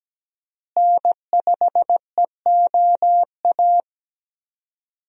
Morse code, heard as a keyed tone, about 17 wpm, 700 Hz. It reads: N5EOA
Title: USA Call Signs - 5 Characters - Repeated 2x - 17wpm